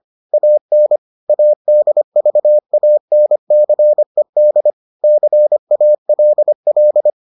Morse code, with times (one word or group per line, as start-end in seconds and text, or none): 0.3-1.0 AN
1.3-4.7 ADVANCED
5.0-7.1 CALL